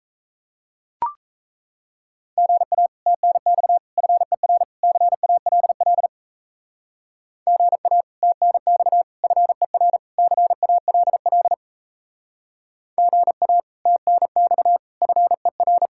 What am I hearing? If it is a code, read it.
Morse code, heard as a keyed tone, about 35 wpm, 700 Hz. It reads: GA TNX FER CALL GA TNX FER CALL GA TNX FER
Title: Speed-Racing Common QSO Phrases Encore - 28wpm